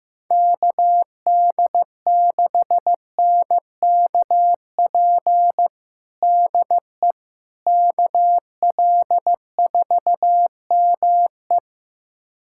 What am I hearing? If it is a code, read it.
Morse code, heard as a keyed tone, about 15 wpm, 700 Hz. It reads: KD6NKP DE KL4ME
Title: USA Callsign DE Callsign 15wpm